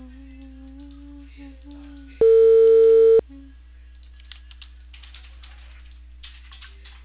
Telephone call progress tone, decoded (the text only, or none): ringback tone